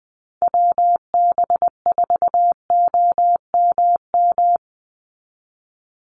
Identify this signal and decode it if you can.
Morse code, heard as a keyed tone, about 20 wpm, 700 Hz. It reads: WB4OMM
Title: Call Signs Repeated 2x - 20wpm